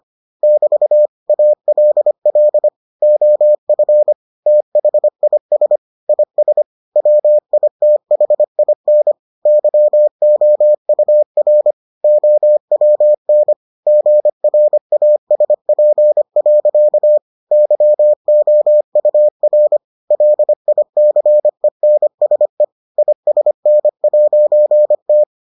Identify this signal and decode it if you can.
Morse code, heard as a keyed tone, about 25 wpm, 600 Hz. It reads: <BT> ALL OF THIS IS WITHIN YOUR OWN GRASP. YOUR LICENSE ISN'T